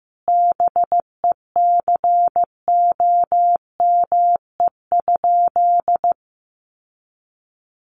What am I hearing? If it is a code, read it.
Morse code, heard as a keyed tone, about 15 wpm, 700 Hz. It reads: BECOME?
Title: Sentences from Top 600 Words 15wpm